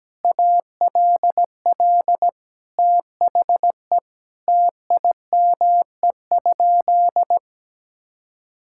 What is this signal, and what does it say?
Morse code, heard as a keyed tone, about 17 wpm, 700 Hz. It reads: ALL THE TIME?
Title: Sentences from Top 1000 Words - Review Part 5 - 17wpm